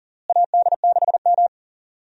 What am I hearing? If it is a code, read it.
Morse code, heard as a keyed tone, about 40 wpm, 700 Hz. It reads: AD6M
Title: USA Call Signs - 4 Characters - Repeated 2x - 40wpm